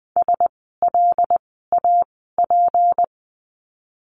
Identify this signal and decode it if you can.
Morse code, heard as a keyed tone, about 20 wpm, spaced as for 15 wpm, 700 Hz. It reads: SLAP
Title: 153 Lesson 25f - Letter p - Words 20wpm